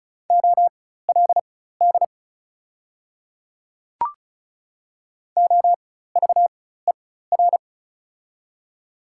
Morse code, heard as a keyed tone, about 36 wpm, spaced as for 16 wpm, 700 Hz. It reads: OLD OVER
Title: 4x Character Spacing - Top 500 Words - 35wpm